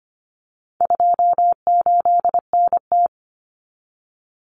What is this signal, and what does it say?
Morse code, heard as a keyed tone, about 25 wpm, 700 Hz. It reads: 28NT
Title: Common Cut Numbers - 4-Digit 25wpm